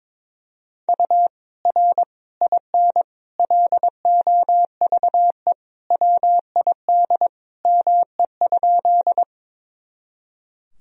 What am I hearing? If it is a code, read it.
Morse code, heard as a keyed tone, about 22 wpm, 700 Hz. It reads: U R IN LOVE WID ME?
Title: Sentences from Top 400 Words and Abbreviations 22wpm